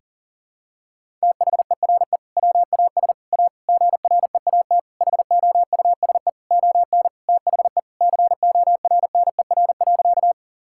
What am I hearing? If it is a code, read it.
Morse code, heard as a keyed tone, about 37 wpm, 700 Hz. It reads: THERE WAS A GREAT HOUSE ON THE CORNER.
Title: Sentences from Top 1600 Words 40wpm